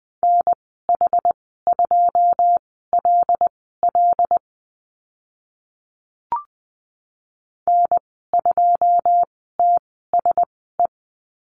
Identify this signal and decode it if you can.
Morse code, heard as a keyed tone, about 20 wpm, spaced as for 14 wpm, 700 Hz. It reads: NH2LL N2TSE
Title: 082 Lesson 14g - Number 2 - Call Signs 20wpm